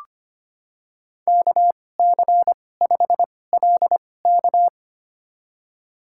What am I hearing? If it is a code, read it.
Morse code, heard as a keyed tone, about 25 wpm, spaced as for 18 wpm, 700 Hz. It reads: KC5LK
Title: USA Call Signs - 2x Character Spacing - 25wpm